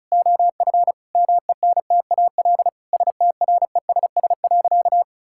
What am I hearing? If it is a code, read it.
Morse code, heard as a keyed tone, about 34 wpm, 700 Hz. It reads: OF MENTAL STRESS.
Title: Sentences from Top 1000 Words - Review Part 6 - 35wpm